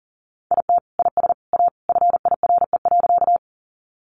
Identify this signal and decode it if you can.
Morse code, heard as a keyed tone, about 40 wpm, 700 Hz. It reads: IT IS A FIRE.
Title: Sentences from Top 400 Words 40wpm